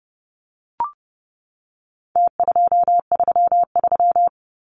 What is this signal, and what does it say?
Morse code, heard as a keyed tone, about 30 wpm, 700 Hz. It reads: T233
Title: Numbers - Contest Style - 4-Digit 30wpm